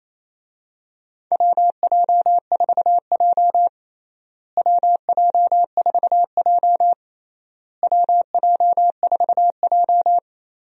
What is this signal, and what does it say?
Morse code, heard as a keyed tone, about 28 wpm, 700 Hz. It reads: WJ4J WJ4J WJ4J
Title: USA Call Signs - 4 Characters - Repeated 3x - 28wpm